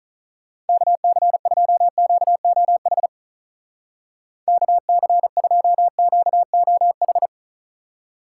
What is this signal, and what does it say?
Morse code, heard as a keyed tone, about 44 wpm, 700 Hz. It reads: KC2QOH KC2QOH
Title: Speed Racing - USA 2x3 Call Signs 35wpm